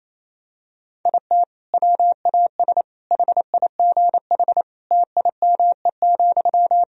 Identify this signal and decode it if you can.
Morse code, heard as a keyed tone, about 28 wpm, 700 Hz. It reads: IT WAS HIGH TIME,
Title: Sentences from Top 1000 Words - Review Part 1 - 28wpm